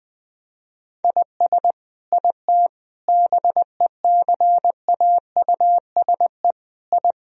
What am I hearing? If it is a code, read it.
Morse code, heard as a keyed tone, about 21 wpm, 700 Hz. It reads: IS IT BECAUSE I
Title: Sentences from Top 1000 Words - Review Part 3 - 20wpm